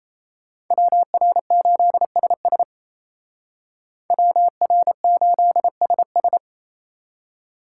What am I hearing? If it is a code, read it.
Morse code, heard as a keyed tone, about 33 wpm, 700 Hz. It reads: WR8SS WR8SS
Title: Speed Racing - USA 2x2 Call Signs 28wpm